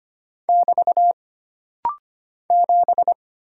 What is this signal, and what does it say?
Morse code, heard as a keyed tone, about 25 wpm, 700 Hz. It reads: <BT> 7
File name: Single Character (Letter:Number:Punctuation:Prosign) - Rapid-Fire - 25wpm